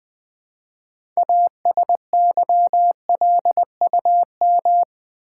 Morse code, heard as a keyed tone, about 20 wpm, 700 Hz. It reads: ASYLUM
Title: Build A Word - 3 Letters At A Time - 6-Letter Words - 20wpm